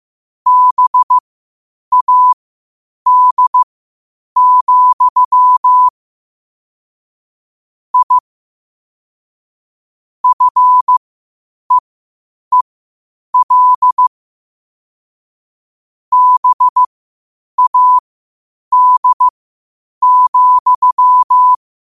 Morse code, heard as a keyed tone, about 15 wpm, spaced as for 8 wpm, 1000 Hz. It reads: BAD, I FEEL BAD,
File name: qotdfiles / 8wpm